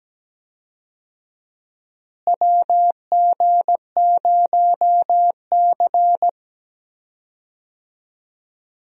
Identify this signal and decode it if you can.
Morse code, heard as a keyed tone, about 17 wpm, 700 Hz. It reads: WG0C
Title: USA Call Signs - 4 Characters - Repeated 2x - 17wpm